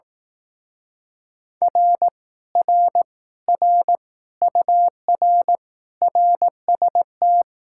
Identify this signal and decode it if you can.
Morse code, heard as a keyed tone, about 18 wpm, 700 Hz. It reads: R R R UR RST
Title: Speed-Racing Common QSO Phrases Encore - 15wpm